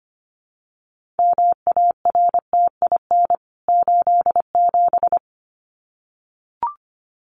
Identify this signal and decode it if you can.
Morse code, heard as a keyed tone, about 25 wpm, 700 Hz. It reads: MARTIN 87
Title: CWT Exchange 25wpm